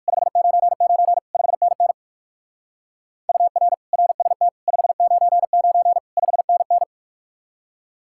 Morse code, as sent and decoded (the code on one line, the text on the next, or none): ..... ----. ----. / ..... -. -. / ..- .-. / .-. ... - / ..... ----. ----. / ..... -. -.
599 5NN UR RST 599 5NN